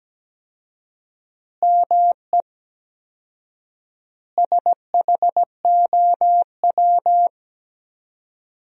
Morse code, heard as a keyed tone, about 17 wpm, 700 Hz. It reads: ME SHOW
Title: 4x Word-Spacing - Sentences from Top 500 Words - 17wpm